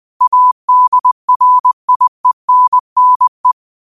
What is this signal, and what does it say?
Morse code, heard as a keyed tone, about 20 wpm, 1000 Hz. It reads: ADRIENNE